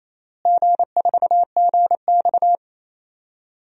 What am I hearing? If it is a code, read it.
Morse code, heard as a keyed tone, about 28 wpm, 700 Hz. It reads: G4GX
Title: DXCC-100 Call Signs - 4 Characters - Repeated 3x - 28wpm